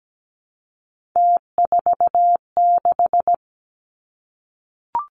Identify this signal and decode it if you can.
Morse code, heard as a keyed tone, about 17 wpm, 700 Hz. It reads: T46
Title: Three Challenging Letter-Number Combinations 17wpm